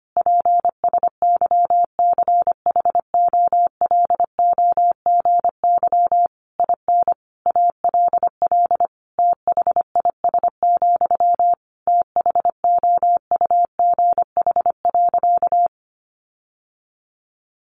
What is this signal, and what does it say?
Morse code, heard as a keyed tone, about 25 wpm, 700 Hz. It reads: PSYCHOLOGY IN ALL THIS, THOUGH.
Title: Sentences from Top 1400 Words 25wpm